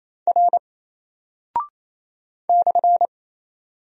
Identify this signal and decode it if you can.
Morse code, heard as a keyed tone, about 28 wpm, 700 Hz. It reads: R /
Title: Single Character (Letter:Number:Punctuation:Prosign) 28wpm